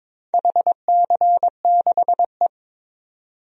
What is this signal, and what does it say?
Morse code, heard as a keyed tone, about 22 wpm, 700 Hz. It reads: HC6E
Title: DXCC-100 Call Signs - 4 Characters - 22wpm